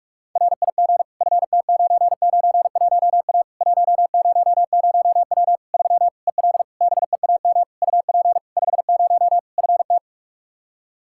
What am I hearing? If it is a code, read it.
Morse code, heard as a keyed tone, about 46 wpm, 700 Hz. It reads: RIG FT991A 100W 3 EL BEAM UP 50 FT
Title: K1USN SST QSO Practice 45wpm